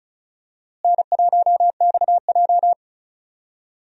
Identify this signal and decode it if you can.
Morse code, heard as a keyed tone, about 36 wpm, 700 Hz. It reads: N1XJ
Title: USA Call Signs - 4 Characters - Repeated 2x - 35wpm